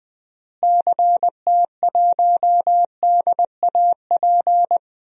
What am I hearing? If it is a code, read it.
Morse code, heard as a keyed tone, about 20 wpm, 700 Hz. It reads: CT1DAP
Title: DXCC-100 Call Signs - 6 Characters - Repeated 3x - 20wpm